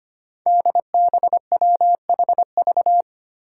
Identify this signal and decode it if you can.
Morse code, heard as a keyed tone, about 25 wpm, 700 Hz. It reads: DBWHV
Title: Five Challenging Letter-Number Combinations 25wpm